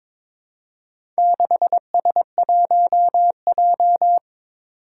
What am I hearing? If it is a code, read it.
Morse code, heard as a keyed tone, about 23 wpm, 700 Hz. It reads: 6S1J
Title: Four Challenging Letter-Number Combinations 22wpm